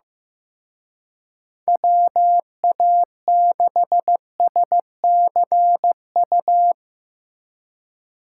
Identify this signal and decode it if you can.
Morse code, heard as a keyed tone, about 16 wpm, 700 Hz. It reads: WA6SCU
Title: Speed Racing - USA 2x3 Call Signs 15wpm